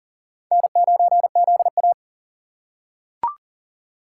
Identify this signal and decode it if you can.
Morse code, heard as a keyed tone, about 37 wpm, 700 Hz. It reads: N9ZA
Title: USA Call Signs - 4 Characters - 3x Character Spacing - 40wpm